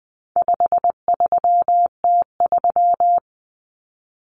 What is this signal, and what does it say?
Morse code, heard as a keyed tone, about 20 wpm, 700 Hz. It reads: 53T3
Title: Common Cut Numbers - 4-Digit 20wpm